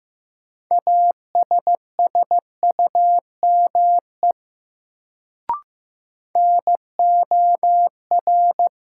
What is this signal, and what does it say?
Morse code, heard as a keyed tone, about 16 wpm, 700 Hz. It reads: ASSUME NOR
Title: Sentences from Top 1500 Words - New Word Drill - 15wpm